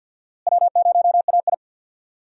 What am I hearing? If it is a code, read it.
Morse code, heard as a keyed tone, about 46 wpm, 700 Hz. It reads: W0AI
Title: USA Call Signs - 4 Characters - Repeated 2x - 50wpm